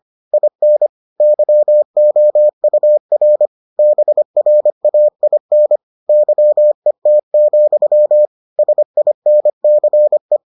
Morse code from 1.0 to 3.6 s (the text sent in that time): YOUR